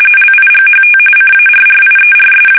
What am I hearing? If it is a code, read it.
RTTY, about 45.45 baud. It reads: LIGHT CARRIERS
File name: fsk850 / battleshipnj